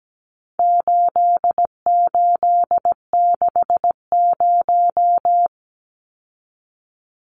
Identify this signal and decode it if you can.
Morse code, heard as a keyed tone, about 17 wpm, 700 Hz. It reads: 8860
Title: Numbers - 4-Digit 17wpm